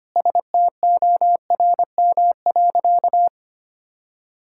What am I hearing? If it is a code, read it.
Morse code, heard as a keyed tone, about 25 wpm, 700 Hz. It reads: STORM.
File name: practice / 25wpm / Sentences with 4-Word Phrases - 25wpm